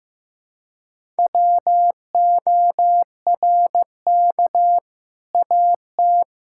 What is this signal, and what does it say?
Morse code, heard as a keyed tone, about 15 wpm, 700 Hz. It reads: WORK AT
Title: Sentences from Top 1000 Words - Review Part 5 - 15wpm